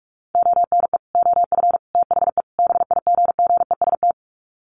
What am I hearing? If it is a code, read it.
Morse code, heard as a keyed tone, about 44 wpm, 700 Hz. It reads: ONE OF THE BIGGEST
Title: Sets of 4 Words 45wpm